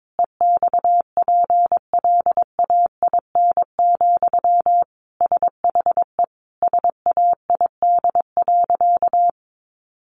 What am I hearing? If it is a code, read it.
Morse code, heard as a keyed tone, about 22 wpm, 700 Hz. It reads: EXPLAIN, SHE SAID.
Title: Sentences from Top 1100 Words 22wpm